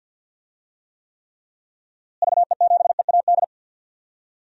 Morse code, heard as a keyed tone, about 49 wpm, 700 Hz. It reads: VE7EAD
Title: DXCC-100 Call Signs - 6 Characters - Repeated 3x - 50wpm